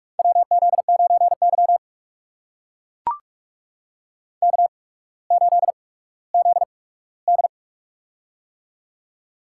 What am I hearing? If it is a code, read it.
Morse code, heard as a keyed tone, about 44 wpm, spaced as for 19 wpm, 690 Hz. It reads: WZ9Y K8ZD
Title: USA Call Signs - 4 Characters - 8x Character Spacing - 45wpm